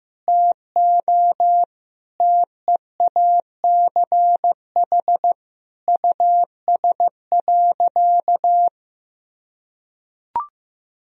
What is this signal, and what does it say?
Morse code, heard as a keyed tone, about 15 wpm, 700 Hz. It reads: TO TEACH US.
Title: Sentences from Top 1000 Words - Review Part 2 - 15wpm